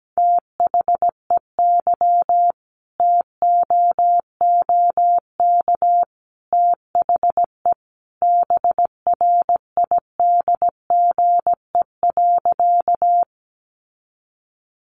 Morse code, heard as a keyed tone, about 17 wpm, 700 Hz. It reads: THEY TOOK THE BRIDGE.
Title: Sentences from Top 1200 Words 17wpm